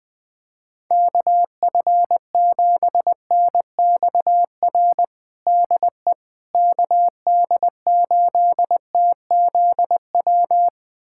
Morse code, heard as a keyed tone, about 20 wpm, 700 Hz. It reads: KF7NXR DE KD8TZW
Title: USA Callsign DE Callsign - 6 Characters - 20wpm